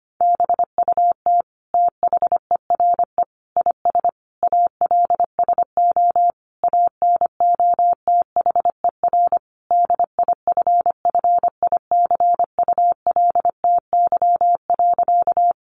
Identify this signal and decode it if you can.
Morse code, heard as a keyed tone, about 25 wpm, 700 Hz. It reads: BUT THERE IS ALSO ANOTHER DIFFICULTY.
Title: Sentences from Top 1500 Words 25wpm